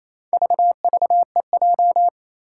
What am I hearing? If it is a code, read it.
Morse code, heard as a keyed tone, about 28 wpm, 700 Hz. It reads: VVEJ
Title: Four Challenging Letter-Number Combinations 28wpm